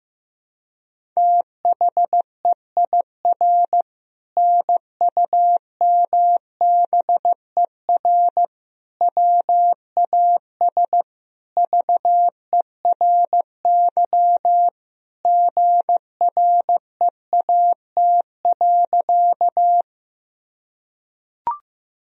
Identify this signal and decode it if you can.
Morse code, heard as a keyed tone, about 15 wpm, 700 Hz. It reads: THEIR NUMBER WAS VERY GREAT.
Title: Sentences from Top 1000 Words - Review Part 1 - 15wpm